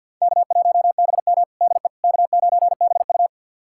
Morse code, heard as a keyed tone, about 52 wpm, 700 Hz. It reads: K1BM DE K9BU
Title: USA Callsign DE Callsign - 4 Characters - 50wpm